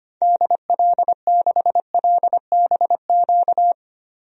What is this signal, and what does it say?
Morse code, heard as a keyed tone, about 25 wpm, 700 Hz. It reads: DL6LBQ
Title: DXCC-100 Call Signs - 6 Characters - Repeated 3x - 25wpm